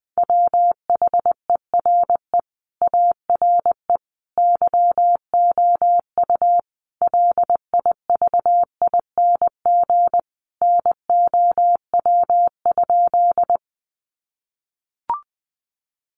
Morse code, heard as a keyed tone, about 20 wpm, 700 Hz. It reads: WHERE ARE YOU LIVING NOW?
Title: Sentences from Top 1100 Words 20wpm